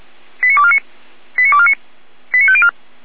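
Five-tone selective call (ZVEI), heard as quick runs of five tones, tones of about 70 ms. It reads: 89249 89249 89504
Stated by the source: ZVEI five-tone call 89123 received by SDR, from feuerwehr-orlamuende.org fire brigade server.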